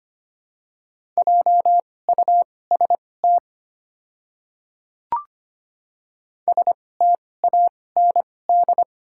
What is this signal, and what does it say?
Morse code, heard as a keyed tone, about 25 wpm, spaced as for 18 wpm, 700 Hz. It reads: JUST STAND
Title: 2x Character Spacing - Top 300 Words - 25wpm